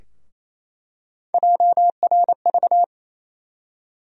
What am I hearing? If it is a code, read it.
Morse code, heard as a keyed tone, about 28 wpm, 700 Hz. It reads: JRV
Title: USA Call Sign Postfixes - 3 Letters 28wpm